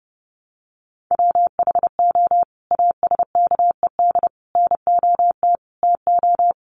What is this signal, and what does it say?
Morse code, heard as a keyed tone, about 30 wpm, 700 Hz. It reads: WHO ASKED NOT TO
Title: Sets of 5 Words Encore2 30wpm